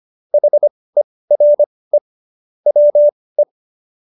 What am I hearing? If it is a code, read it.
Morse code, heard as a keyed tone, about 25 wpm, spaced as for 18 wpm, 590 Hz. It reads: HERE WE
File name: qso222 18